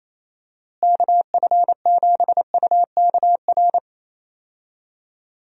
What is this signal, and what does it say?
Morse code, heard as a keyed tone, about 28 wpm, 700 Hz. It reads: KF7UKR
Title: USA Call Signs - 6 Characters - Repeated 2x - 28wpm